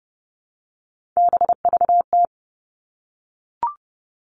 Morse code, heard as a keyed tone, about 30 wpm, 700 Hz. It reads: BVT
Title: All Cut Numbers - 3-Digit 30wpm